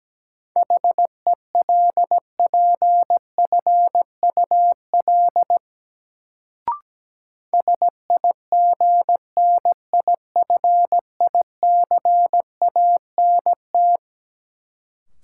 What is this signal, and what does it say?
Morse code, heard as a keyed tone, about 17 wpm, 700 Hz. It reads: HELPFUL SIGNIFICANT
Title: Sentences from Top 1000 Words - Review Part 6 - New Word Drill - 17wpm